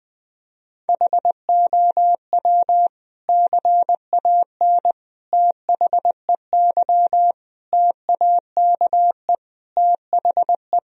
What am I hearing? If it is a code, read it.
Morse code, heard as a keyed tone, about 21 wpm, 700 Hz. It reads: HOW CAN THEY TAKE THE
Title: Sentences from Top 1000 Words - Review Part 3 - 20wpm